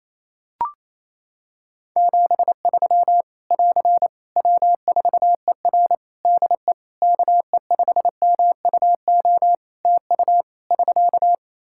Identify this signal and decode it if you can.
Morse code, heard as a keyed tone, about 27 wpm, 700 Hz. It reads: 73 <AR> W4ER DE KE5MUO TU <SK>
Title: K1USN SST QSO Practice 28wpm